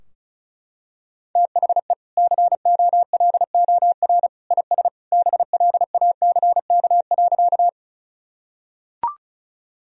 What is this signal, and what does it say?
Morse code, heard as a keyed tone, about 36 wpm, 700 Hz. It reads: THE COLOR IS BLACK.
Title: Sentences from Top 1000 Words - Review Part 2 - 35wpm